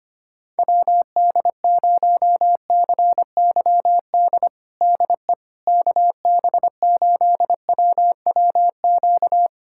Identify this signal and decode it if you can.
Morse code, heard as a keyed tone, about 25 wpm, 700 Hz. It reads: WD0CYD DE KB8WWQ
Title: USA Callsign DE Callsign - 6 Characters - 25wpm